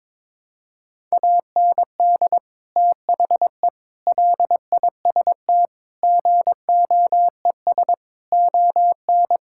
Morse code, heard as a keyed tone, about 21 wpm, 700 Hz. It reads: AND THE LIST GOES ON
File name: Sentences with 7-Word Phrases - 22wpm